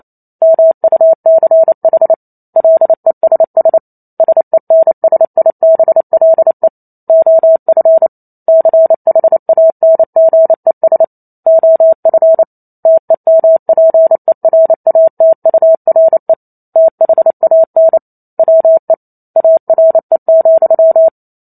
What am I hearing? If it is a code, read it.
Morse code, heard as a keyed tone, about 28 wpm, 650 Hz. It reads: MUCH LESS SENSIBLE OF CHANGES OF TEMPERATURE THAN WE ARE,